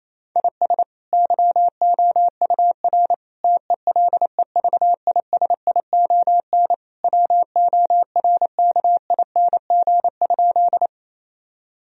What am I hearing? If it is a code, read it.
Morse code, heard as a keyed tone, about 28 wpm, 700 Hz. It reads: IS YOUR TELEVISION WORKING?
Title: Sentences from Top 1000 Words - Review Part 4 - 28wpm